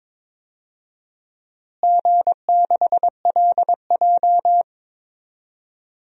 Morse code, heard as a keyed tone, about 21 wpm, 700 Hz. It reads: G6LJ
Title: DXCC-100 Call Signs - 4 Characters - Repeated 3x - 22wpm